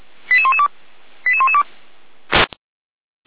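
Five-tone selective call (ZVEI), heard as quick runs of five tones, tones of about 70 ms. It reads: 88112 88112
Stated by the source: ZVEI five-tone call 88519 received on SDR.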